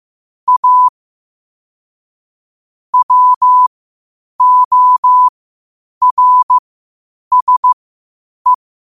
Morse code, heard as a keyed tone, about 15 wpm, spaced as for 8 wpm, 1000 Hz. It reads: A WORSE
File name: qotdfiles / 8wpm